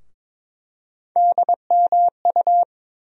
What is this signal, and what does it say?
Morse code, heard as a keyed tone, about 21 wpm, 700 Hz. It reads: DMU
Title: USA Call Sign Postfixes - 3 Letters 22wpm